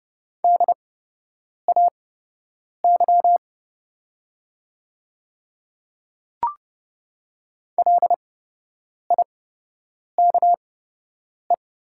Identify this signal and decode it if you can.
Morse code, heard as a keyed tone, about 30 wpm, spaced as for 8 wpm, 700 Hz. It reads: DAY LIKE
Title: 8x Character Spacing - Top 200 Words - 30wpm